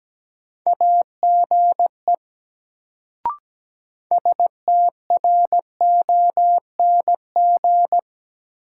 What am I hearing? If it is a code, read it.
Morse code, heard as a keyed tone, about 17 wpm, 700 Hz. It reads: AGE STRONG
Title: Sentences from Top 500 Words - New Word Drill - 17wpm